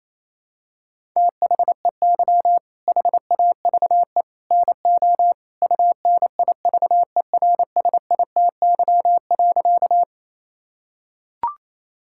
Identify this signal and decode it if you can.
Morse code, heard as a keyed tone, about 28 wpm, 700 Hz. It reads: THEY HAVE NO UNIVERSITY.
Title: Sentences from Top 1000 Words - Review Part 5 - 28wpm